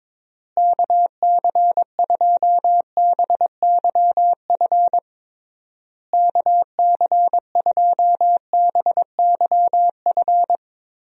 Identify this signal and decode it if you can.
Morse code, heard as a keyed tone, about 22 wpm, 700 Hz. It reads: KC2BYF KC2BYF
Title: USA Call Signs - 6 Characters - Repeated 3x - 22wpm